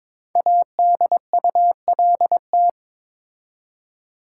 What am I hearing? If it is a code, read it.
Morse code, heard as a keyed tone, about 22 wpm, 700 Hz. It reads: ADULT
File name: 5 Letter Words - First 3 Letters Given - 22wpm